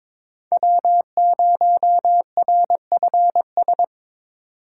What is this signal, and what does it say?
Morse code, heard as a keyed tone, about 22 wpm, 700 Hz. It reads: W0RFS
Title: USA Call Signs - 5 Characters - 2x Character Spacing - 22wpm